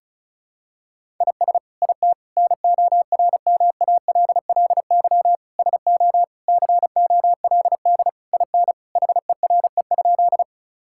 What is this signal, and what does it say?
Morse code, heard as a keyed tone, about 36 wpm, 700 Hz. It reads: IS IT NORMALLY SO COLD IN HERE?
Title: Sentences from Top 1000 Words - Review Part 6 - 35wpm